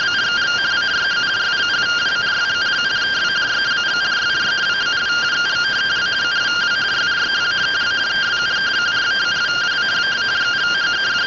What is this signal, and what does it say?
Two-tone teleprinter signal, about 170 Hz shift.
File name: findings / axial iodine 10mm 184.gen-02 rtty rx